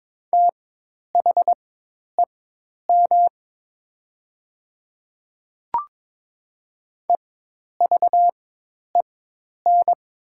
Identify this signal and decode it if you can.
Morse code, heard as a keyed tone, about 22 wpm, spaced as for 10 wpm, 700 Hz. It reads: THEM EVEN
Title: 4x Character Spacing - Top 200 Words - 22wpm